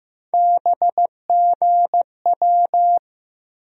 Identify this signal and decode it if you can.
Morse code, heard as a keyed tone, about 15 wpm, 700 Hz. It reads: BGW
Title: USA Call Sign Postfixes - 3 Letters 15wpm